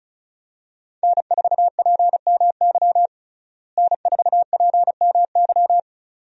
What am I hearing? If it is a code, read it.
Morse code, heard as a keyed tone, about 34 wpm, 700 Hz. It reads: N4PMY N4PMY
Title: USA Call Signs - 5 Characters - Repeated 2x - 35wpm